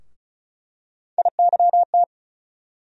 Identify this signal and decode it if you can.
Morse code, heard as a keyed tone, about 36 wpm, 700 Hz. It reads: IYT
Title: USA Call Sign Postfixes - 3 Letters 35wpm